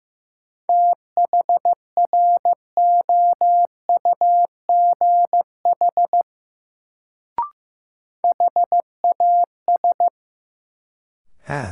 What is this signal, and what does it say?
Morse code, heard as a keyed tone, about 15 wpm, 700 Hz. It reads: THROUGH HAS
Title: Sets of 2 Words - New Word Drill - 15wpm